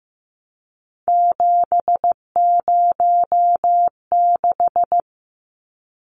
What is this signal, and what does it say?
Morse code, heard as a keyed tone, about 15 wpm, 700 Hz. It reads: 706
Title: Numbers - 3-Digit 15wpm